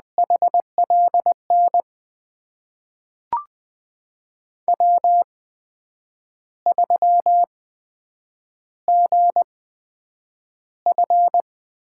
Morse code, heard as a keyed tone, about 20 wpm, spaced as for 8 wpm, 700 Hz. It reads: HLN W3GF
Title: USA Call Signs - 8x Character Spacing - 20wpm